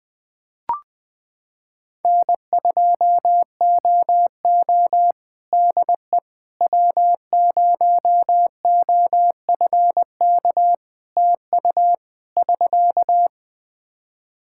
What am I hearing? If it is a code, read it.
Morse code, heard as a keyed tone, about 20 wpm, 700 Hz. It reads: N2OO DE W0OFK TU <SK>
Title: K1USN SST QSO Practice 20wpm